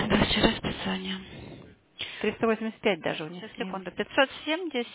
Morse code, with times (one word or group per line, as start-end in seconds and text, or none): none